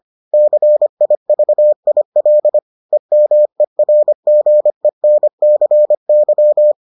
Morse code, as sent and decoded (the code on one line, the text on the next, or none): -.-. .. ...- .. .-.. / . -- . .-. --. -. -.-. -.--
CIVIL EMERGENCY